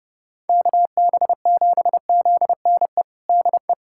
Morse code, heard as a keyed tone, about 30 wpm, 700 Hz. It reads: KB7ZNE DE
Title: USA Callsign DE Callsign 30wpm